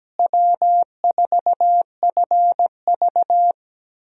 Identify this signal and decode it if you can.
Morse code, heard as a keyed tone, about 17 wpm, 700 Hz. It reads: W4FV
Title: USA Call Signs - 4 Characters - Repeated 2x - 17wpm